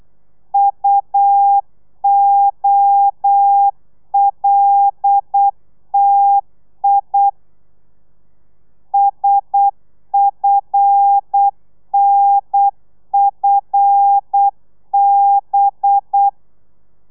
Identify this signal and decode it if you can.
Morse code, heard as a keyed tone, about 8 wpm, 800 Hz. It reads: UOLTI SFNFB
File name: telegrafia / i2tfj / Lez5 35